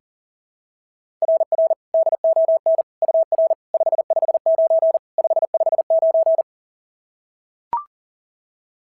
Morse code, heard as a keyed tone, about 37 wpm, 660 Hz. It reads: RR DON UR 559 559
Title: SOTA QSO Practice 40wpm